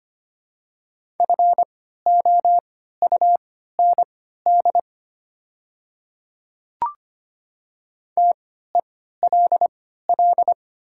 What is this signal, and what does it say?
Morse code, heard as a keyed tone, about 25 wpm, spaced as for 14 wpm, 700 Hz. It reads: FOUND TELL
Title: 3x Character Spacing - Top 300 Words - 25wpm